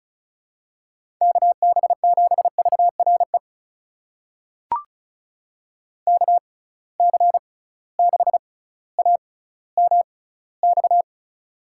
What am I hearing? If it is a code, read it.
Morse code, heard as a keyed tone, about 36 wpm, spaced as for 18 wpm, 700 Hz. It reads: KB7VRE KC6AMX
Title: USA Call Signs - 6 Characters - 6x Character Spacing - 35wpm